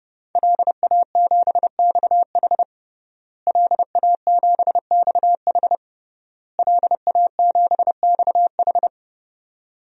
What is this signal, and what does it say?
Morse code, heard as a keyed tone, about 29 wpm, 700 Hz. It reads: LA7XH LA7XH LA7XH
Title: DXCC-100 Call Signs - 5 Characters - Repeated 3x - 30wpm